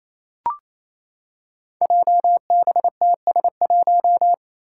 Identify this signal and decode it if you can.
Morse code, heard as a keyed tone, about 28 wpm, 700 Hz. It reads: JBTS1
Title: Five Challenging Letter-Number Combinations 28wpm